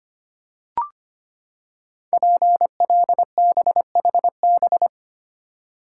Morse code, heard as a keyed tone, about 25 wpm, 700 Hz. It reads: PLBHB
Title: Code Groups - 5 Letter:Numbers - 25wpm